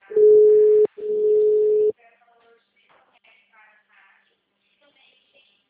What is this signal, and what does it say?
Ringback tone (Europe, 425 Hz), with speech.
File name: out- -2016- - - .27704